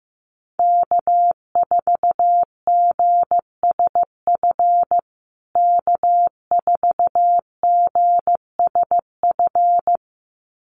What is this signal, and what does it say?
Morse code, heard as a keyed tone, about 15 wpm, 700 Hz. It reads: K4GSF K4GSF
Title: USA Call Signs Repeated 2x 15wpm